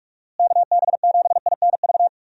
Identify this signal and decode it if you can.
Morse code, heard as a keyed tone, about 41 wpm, 700 Hz. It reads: KB7INV
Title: USA Call Signs - 6 Characters - Repeated 2x - 45wpm